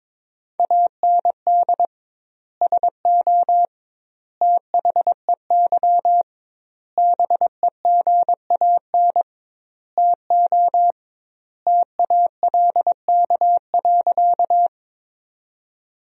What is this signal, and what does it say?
Morse code, heard as a keyed tone, about 22 wpm, 700 Hz. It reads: AND SO THEY BEGAN TO TALK.
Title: 2x Word-Spacing - Sentences from Top 400 Words - 22wpm